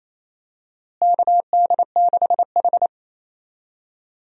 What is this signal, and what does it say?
Morse code, heard as a keyed tone, about 28 wpm, 700 Hz. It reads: KD6H
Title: USA Call Signs - 4 Characters - Repeated 3x - 28wpm